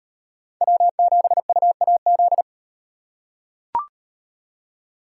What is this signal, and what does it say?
Morse code, heard as a keyed tone, about 38 wpm, 700 Hz. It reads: W7UAZ